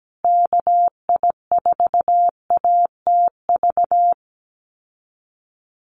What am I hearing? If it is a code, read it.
Morse code, heard as a keyed tone, about 17 wpm, 700 Hz. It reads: KI4ATV
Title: USA Call Signs Repeated 3x 17wpm